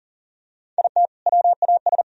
Morse code, heard as a keyed tone, about 40 wpm, 700 Hz. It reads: IT WAS
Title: Sentences from Top 1000 Words - Review Part 1 - 40wpm